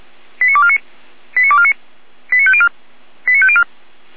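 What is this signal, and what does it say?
Five-tone selective call (ZVEI), heard as quick runs of five tones, tones of about 70 ms. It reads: 89249 89249 89504 89504
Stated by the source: ZVEI five-tone selective call 89250, received by SDR, from the Feuerwehr Orlamünde fire brigade audio server.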